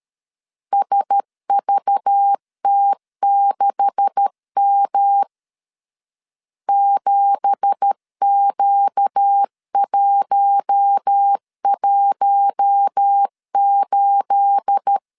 Morse code, heard as a keyed tone, about 13 wpm, 800 Hz. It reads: SVT6M 7Q118
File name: CW / DET / 1 9 1500C S C 12wpm 3811